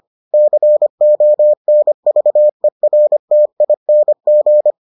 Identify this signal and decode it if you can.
Morse code, heard as a keyed tone, about 25 wpm, 600 Hz. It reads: CONVERTING